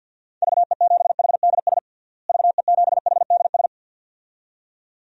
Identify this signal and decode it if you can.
Morse code, heard as a keyed tone, about 51 wpm, 700 Hz. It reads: VE7HDS VE7HDS
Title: DXCC-100 Call Signs - 6 Characters - Repeated 2x - 50wpm